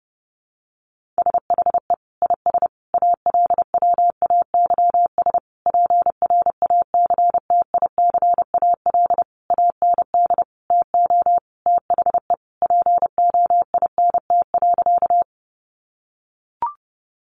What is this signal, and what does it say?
Morse code, heard as a keyed tone, about 30 wpm, 700 Hz. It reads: SHE IS ALWAYS PRACTICAL AND TO THE POINT.
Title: Sentences from Top 1400 Words 30wpm